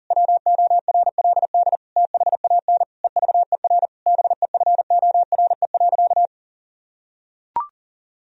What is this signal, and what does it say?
Morse code, heard as a keyed tone, about 40 wpm, 700 Hz. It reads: WORLD THAN EVER BEFORE.
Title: Sentences from Top 1000 Words - Review Part 3 - 40wpm